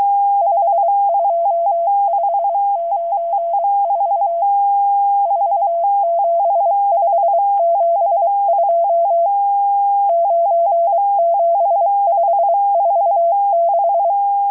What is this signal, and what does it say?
Morse code, heard as a keyed tone, about 23 wpm, 680 Hz. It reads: 52594 47572 97546
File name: HM02- -0437utc-7351khz